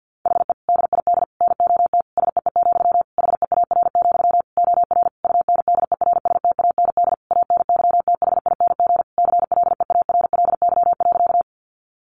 Morse code, heard as a keyed tone, about 48 wpm, 700 Hz. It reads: HE DID NOT SEE, HEAR, OR UNDERSTAND ANYTHING CLEARLY.
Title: Sentences from Top 1300 Words 50wpm